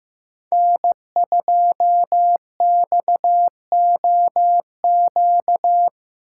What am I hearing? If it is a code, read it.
Morse code, heard as a keyed tone, about 17 wpm, 700 Hz. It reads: N2XOQ
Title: Speed Racing - USA Call Signs 15wpm